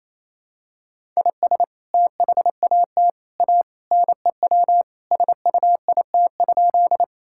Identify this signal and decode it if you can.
Morse code, heard as a keyed tone, about 28 wpm, 700 Hz. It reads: IS THAT A NEW SUIT?
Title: Sentences from Top 1600 Words 28wpm